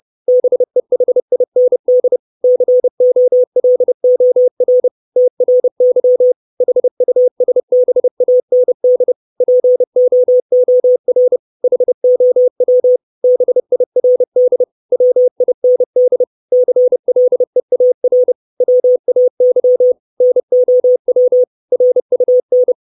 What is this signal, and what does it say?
Morse code, heard as a keyed tone, about 30 wpm, 500 Hz. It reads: BEHIND COLOR TRY HUSBAND POOR HOW BIRD WIND CLEAR WAY NOW RUN
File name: 632 Words 30WPM